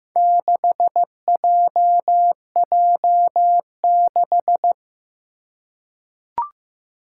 Morse code, heard as a keyed tone, about 15 wpm, 700 Hz. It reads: 6JJ6
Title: Four Challenging Letter-Number Combinations 15wpm